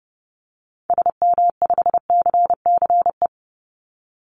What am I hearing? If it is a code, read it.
Morse code, heard as a keyed tone, about 30 wpm, 700 Hz. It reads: SM5CCE
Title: Call Signs Repeated 3x - 30wpm